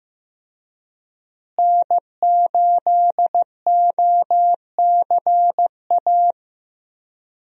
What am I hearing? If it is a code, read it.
Morse code, heard as a keyed tone, about 15 wpm, 700 Hz. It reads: N8OCA
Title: USA Call Signs - 5 Characters - Repeated 3x - 15wpm